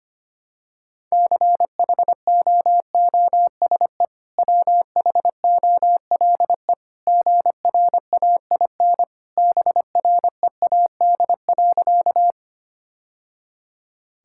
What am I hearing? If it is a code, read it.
Morse code, heard as a keyed tone, about 25 wpm, 700 Hz. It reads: CHOOSE WHOLE GRAIN BREAD.
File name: Easy Sentences with Hints - 25wpm